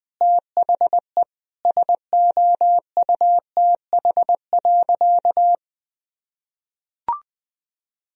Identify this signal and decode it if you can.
Morse code, heard as a keyed tone, about 20 wpm, 700 Hz. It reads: THE SOUTH.
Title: Sentences from Top 1000 Words - Review Part 4 - 20wpm